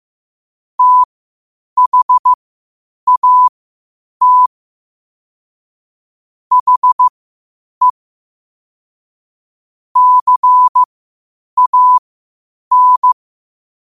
Morse code, heard as a keyed tone, about 15 wpm, spaced as for 8 wpm, 1000 Hz. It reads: THAT HE CAN
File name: qotdfiles / 8wpm